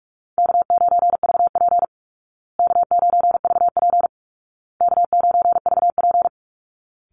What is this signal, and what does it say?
Morse code, heard as a keyed tone, about 44 wpm, 700 Hz. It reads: K9VP K9VP K9VP
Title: Call Signs Repeated 2x - 45wpm